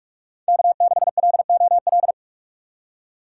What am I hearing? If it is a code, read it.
Morse code, heard as a keyed tone, about 44 wpm, 700 Hz. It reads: K6LOL
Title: USA Call Signs - 4x Character Spacing - 45wpm